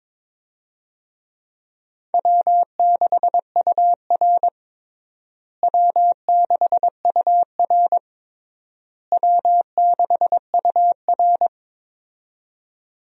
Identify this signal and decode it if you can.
Morse code, heard as a keyed tone, about 21 wpm, 700 Hz. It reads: W6UR W6UR W6UR
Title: USA Call Signs - 4 Characters - Repeated 3x - 22wpm